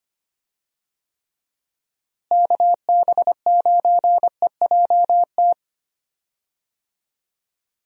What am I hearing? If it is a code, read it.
Morse code, heard as a keyed tone, about 25 wpm, 700 Hz. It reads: KB9EJT
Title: USA Call Signs - 6 Characters - Repeated 3x - 25wpm